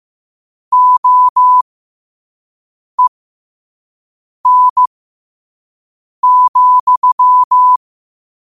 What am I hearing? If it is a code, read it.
Morse code, heard as a keyed tone, about 15 wpm, spaced as for 5 wpm, 1000 Hz. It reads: OEN,
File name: 5wpm